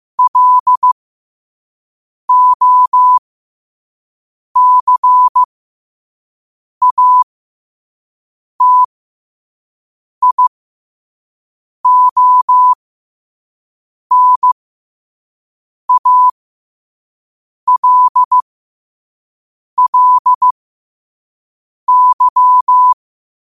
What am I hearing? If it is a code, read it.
Morse code, heard as a keyed tone, about 15 wpm, spaced as for 5 wpm, 1000 Hz. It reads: LOCATIONALLY
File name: qotdfiles / 5wpm